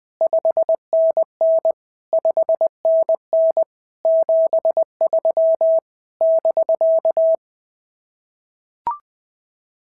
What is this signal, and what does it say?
Morse code, heard as a keyed tone, about 20 wpm, 650 Hz. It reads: 5NN 5NN 73 <BK>
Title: SOTA QSO Practice 20wpm